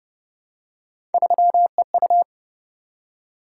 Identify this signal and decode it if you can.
Morse code, heard as a keyed tone, about 28 wpm, 700 Hz. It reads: 3EU